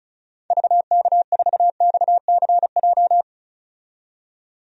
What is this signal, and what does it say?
Morse code, heard as a keyed tone, about 36 wpm, 700 Hz. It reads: VK4XCJ
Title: DXCC-100 Call Signs - 6 Characters - 35wpm